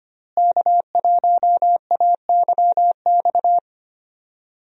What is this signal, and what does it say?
Morse code, heard as a keyed tone, about 25 wpm, 700 Hz. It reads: K1AYX
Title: USA Call Signs - 5 Characters - Repeated 3x - 25wpm